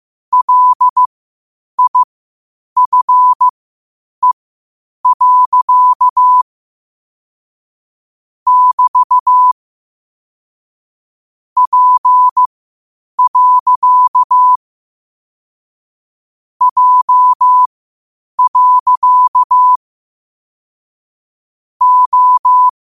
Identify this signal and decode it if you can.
Morse code, heard as a keyed tone, about 15 wpm, spaced as for 8 wpm, 1000 Hz. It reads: LIFE. <BT> P. J. O
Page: Quotes for Fri, 15 Aug 2025 in Morse Code at 8 words per minute.